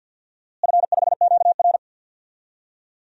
Morse code, heard as a keyed tone, about 49 wpm, 700 Hz. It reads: F5QR
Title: DXCC-100 Call Signs - 4 Characters - Repeated 2x - 50wpm